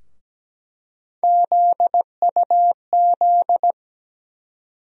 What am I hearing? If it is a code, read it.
Morse code, heard as a keyed tone, about 17 wpm, 700 Hz. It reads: ZUZ
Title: USA Call Sign Postfixes - 3 Letters 17wpm